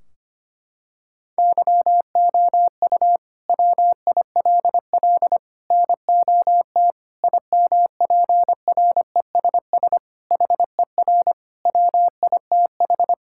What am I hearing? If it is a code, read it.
Morse code, heard as a keyed tone, about 25 wpm, 700 Hz. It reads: YOU WILL NOT IMPRESS HER WITH
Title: Sentences from Top 1600 Words 25wpm